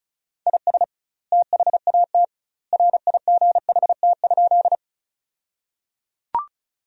Morse code, heard as a keyed tone, about 34 wpm, 700 Hz. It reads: IS THAT RIGHT?
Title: 2x Word-Spacing - Sentences from Top 200 Words - 35wpm